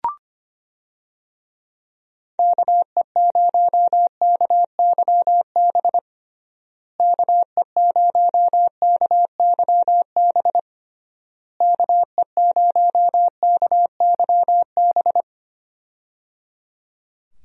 Morse code, heard as a keyed tone, about 25 wpm, 700 Hz. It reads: KE0KYB KE0KYB KE0KYB
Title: USA Call Signs - 6 Characters - Repeated 3x - 25wpm